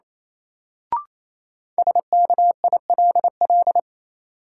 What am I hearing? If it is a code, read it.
Morse code, heard as a keyed tone, about 28 wpm, 700 Hz. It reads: SKILL
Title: Sentences from Top 1200 Words - New Word Drill - 28wpm